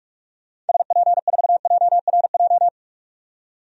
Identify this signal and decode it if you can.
Morse code, heard as a keyed tone, about 42 wpm, 700 Hz. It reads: SP4JRJ
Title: DXCC-100 Call Signs - 6 Characters - 45wpm